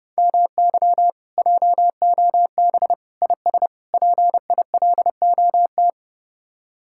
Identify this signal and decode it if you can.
Morse code, heard as a keyed tone, about 37 wpm, 700 Hz. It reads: MY JOB IS PILOT
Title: Speed-Racing Common QSO Phrases Encore - 30wpm